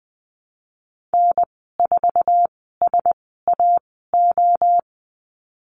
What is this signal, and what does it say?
Morse code, heard as a keyed tone, about 20 wpm, spaced as for 15 wpm, 700 Hz. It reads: N4SAO
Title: 088 Lesson 15g - Number 5 - Call Signs 20wpm